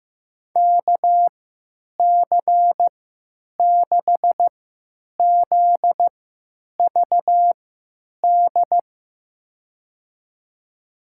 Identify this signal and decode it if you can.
Morse code, heard as a keyed tone, about 15 wpm, spaced as for 9 wpm, 700 Hz. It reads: KC6ZVD
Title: USA Call Signs - 6 Characters - 3x Character Spacing - 15wpm